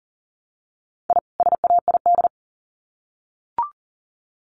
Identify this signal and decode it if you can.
Morse code, heard as a keyed tone, about 40 wpm, 700 Hz. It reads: I SAID
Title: Sets of 2 Words 40wpm